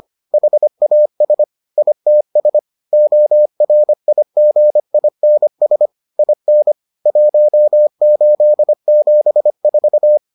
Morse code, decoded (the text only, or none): HAS ITS ORIGINS IN 1874